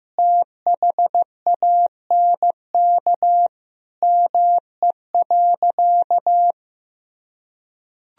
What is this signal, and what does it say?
Morse code, heard as a keyed tone, about 16 wpm, 700 Hz. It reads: THANK ME.
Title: Sentences from Top 1000 Words - Review Part 1 - 15wpm